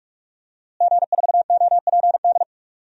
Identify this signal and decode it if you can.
Morse code, heard as a keyed tone, about 44 wpm, 700 Hz. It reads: G4OPD
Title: DXCC-100 Call Signs - 5 Characters - Repeated 3x - 45wpm